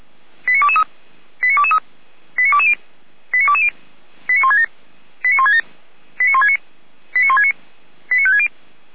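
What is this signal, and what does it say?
Five-tone selective call (ZVEI), heard as quick runs of five tones, tones of about 70 ms. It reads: 89223 89223 89220 89220 89167 89167 89169 89169 89560
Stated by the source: ZVEI five-tone selective call 89163 received by SDR.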